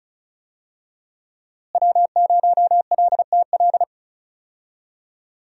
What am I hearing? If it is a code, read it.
Morse code, heard as a keyed tone, about 36 wpm, 700 Hz. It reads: W0LTL
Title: USA Call Signs - 5 Characters - Repeated 3x - 35wpm